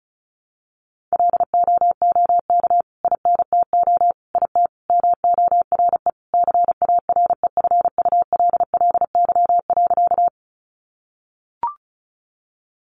Morse code, heard as a keyed tone, about 34 wpm, 700 Hz. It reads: LOOK INTO IT MORE CAREFULLY.
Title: Sentences from Top 1300 Words 35wpm